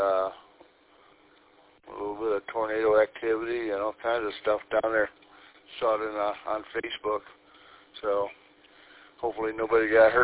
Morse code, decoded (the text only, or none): none